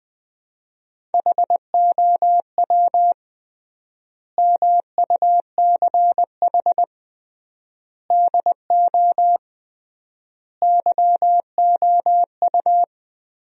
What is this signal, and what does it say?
Morse code, heard as a keyed tone, about 21 wpm, 700 Hz. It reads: HOW MUCH DO YOU
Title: 3x Word-Spacing - Sentences from Top 200 Words - 20wpm